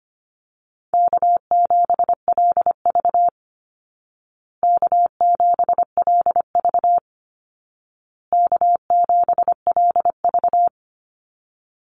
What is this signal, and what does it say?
Morse code, heard as a keyed tone, about 25 wpm, 700 Hz. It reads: K7LV K7LV K7LV
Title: Call Signs Repeated 2x - 25wpm